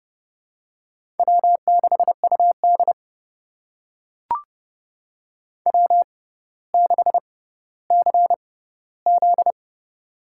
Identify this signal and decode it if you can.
Morse code, heard as a keyed tone, about 29 wpm, spaced as for 15 wpm, 700 Hz. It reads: W6UD W6CZ
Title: USA Call Signs - 6x Character Spacing - 30wpm